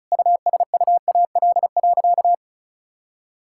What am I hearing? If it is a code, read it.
Morse code, heard as a keyed tone, about 34 wpm, 690 Hz. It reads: USUAL.
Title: Sentences from Top 1000 Words - Review Part 5 - 35wpm